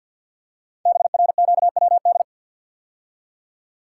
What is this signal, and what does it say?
Morse code, heard as a keyed tone, about 48 wpm, 700 Hz. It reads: BRQWD
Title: Code Groups - 5 Letters - 50wpm